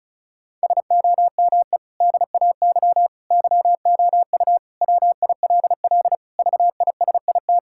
Morse code, heard as a keyed tone, about 36 wpm, 700 Hz. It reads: SOME DAY YOU WILL VISIT
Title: Sentences from Top 1000 Words - Review Part 6 - 35wpm